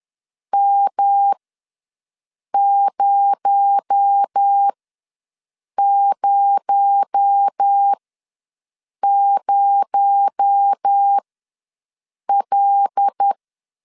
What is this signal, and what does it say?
Morse code, heard as a keyed tone, about 10 wpm, spaced as for 6 wpm, 800 Hz. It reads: M000L
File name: LEZIONI / 6 PKX0 / 6 PKX0 ws 10wpm 4220